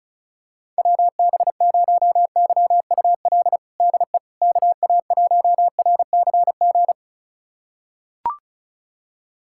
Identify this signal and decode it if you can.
Morse code, heard as a keyed tone, about 36 wpm, 690 Hz. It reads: WB0YUL DE KA1RCG
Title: USA Callsign DE Callsign 35wpm